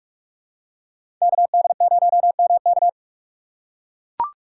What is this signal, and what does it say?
Morse code, heard as a keyed tone, about 43 wpm, 700 Hz. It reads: KD0MK
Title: USA Call Signs - 5 Characters - 2x Character Spacing - 45wpm